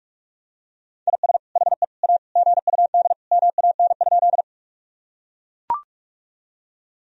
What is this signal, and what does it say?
Morse code, heard as a keyed tone, about 41 wpm, 700 Hz. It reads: IS HE A GUD MAN?
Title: Sentences from Top 200 Words and Abbreviations 45wpm